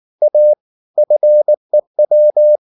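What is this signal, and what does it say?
Morse code, heard as a keyed tone, about 19 wpm, 600 Hz. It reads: A FEW
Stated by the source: Morse code at 19 wpm from CW med Gnister, reading Ken Follett's novel Never in English.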